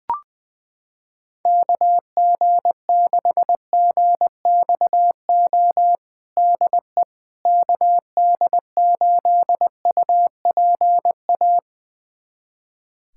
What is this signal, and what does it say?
Morse code, heard as a keyed tone, about 21 wpm, 700 Hz. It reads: KG6GXO DE KD8UPA
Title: USA Callsign DE Callsign - 6 Characters - 20wpm